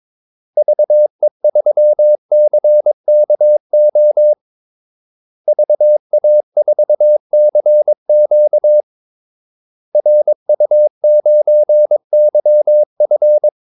Morse code, heard as a keyed tone, about 22 wpm, 600 Hz. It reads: VE3CKO VA4CQ RU9YF
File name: Callsigns-22wpm